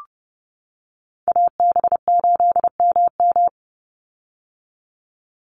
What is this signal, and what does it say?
Morse code, heard as a keyed tone, about 30 wpm, 700 Hz. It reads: AB8MM
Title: USA Call Signs 30wpm